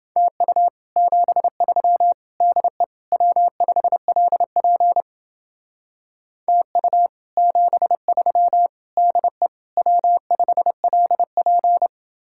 Morse code, heard as a keyed tone, about 29 wpm, 700 Hz. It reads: TU 73 DE W5LP TU 73 DE W5LP